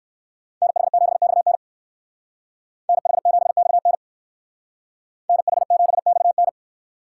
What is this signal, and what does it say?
Morse code, heard as a keyed tone, about 67 wpm, 700 Hz. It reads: NH7XN NH7XN NH7XN